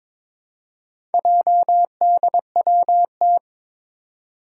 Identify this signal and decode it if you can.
Morse code, heard as a keyed tone, about 22 wpm, 700 Hz. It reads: JDWT
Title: Four Challenging Letter-Number Combinations 22wpm